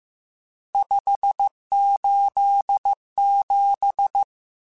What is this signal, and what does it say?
Morse code, heard as a keyed tone, about 15 wpm, 800 Hz. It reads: 587
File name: RSTs / RST-mp3s / Various RST @13